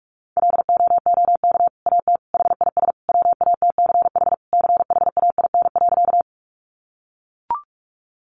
Morse code, heard as a keyed tone, about 44 wpm, 700 Hz. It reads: LOOK AT HIS WATCH CHAIN.
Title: Sentences from Top 1500 Words 45wpm